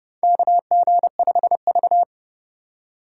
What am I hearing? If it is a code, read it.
Morse code, heard as a keyed tone, about 32 wpm, 700 Hz. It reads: KG5V
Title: K1USN SST QSO Practice - 30wpm